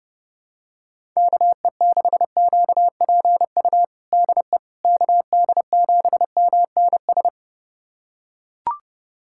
Morse code, heard as a keyed tone, about 30 wpm, 700 Hz. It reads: KE6QPU DE KD7MNS
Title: USA Callsign DE Callsign 30wpm